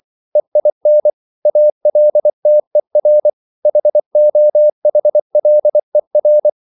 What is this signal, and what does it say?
Morse code, heard as a keyed tone, about 24 wpm, 600 Hz. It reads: EIN ALTER HOHLER